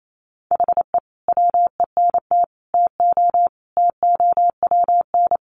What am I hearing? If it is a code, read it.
Morse code, heard as a keyed tone, about 28 wpm, 700 Hz. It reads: HE WENT TO TOWN
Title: Sentences from Top 1300 Words 28wpm